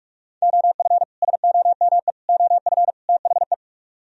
Morse code, heard as a keyed tone, about 44 wpm, 700 Hz. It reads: OF SOME OF THE